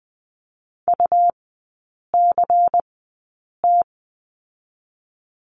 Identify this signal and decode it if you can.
Morse code, heard as a keyed tone, about 20 wpm, spaced as for 8 wpm, 700 Hz. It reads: UCT
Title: 138 Lesson 23e - Letter f - Three Characters 20wpm